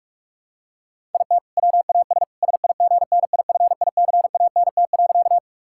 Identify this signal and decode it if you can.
Morse code, heard as a keyed tone, about 46 wpm, 700 Hz. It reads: IT WAS SIGNIFICANT.
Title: Sentences from Top 1000 Words - Review Part 6 - 45wpm